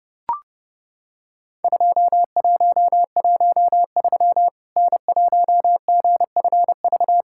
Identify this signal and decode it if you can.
Morse code, heard as a keyed tone, about 32 wpm, 700 Hz. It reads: 2113 N1GFV
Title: Worked All Europe DX Contest - QTC Exchange - 30wpm